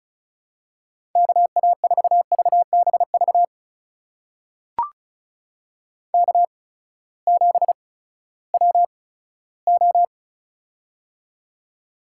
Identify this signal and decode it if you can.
Morse code, heard as a keyed tone, about 34 wpm, spaced as for 17 wpm, 700 Hz. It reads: KA4VBV K7WO
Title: USA Call Signs - 8x Character Spacing - 35wpm